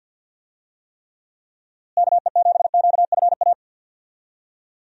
Morse code, heard as a keyed tone, about 46 wpm, 700 Hz. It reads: KE7QFA
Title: USA Call Signs - 6 Characters - Repeated 2x - 50wpm